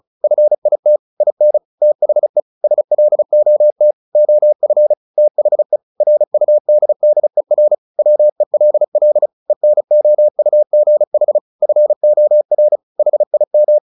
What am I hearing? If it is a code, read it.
Morse code, heard as a keyed tone, about 34 wpm, 600 Hz. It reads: FIT IN THE SLOT OF THE RUDDER WELL ENOUGH FOR HIM
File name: the old man and the sea 2 35wpm